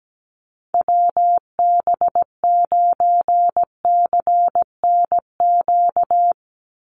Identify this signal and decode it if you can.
Morse code, heard as a keyed tone, about 17 wpm, 700 Hz. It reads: WB9CNQ
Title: USA Call Signs 17wpm